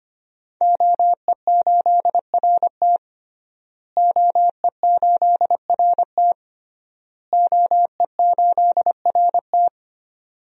Morse code, heard as a keyed tone, about 25 wpm, 700 Hz. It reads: OE8RT OE8RT OE8RT
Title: DXCC-100 Call Signs - 5 Characters - Repeated 3x - 25wpm